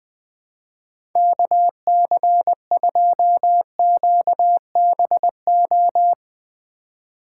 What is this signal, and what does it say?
Morse code, heard as a keyed tone, about 20 wpm, 700 Hz. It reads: KC2QBO
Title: USA Call Signs - 6 Characters - 4x Character Spacing - 20wpm